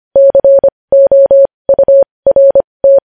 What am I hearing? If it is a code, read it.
Morse code, heard as a keyed tone, about 25 wpm, spaced as for 20 wpm, 550 Hz. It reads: COURT
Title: News Headlines at 20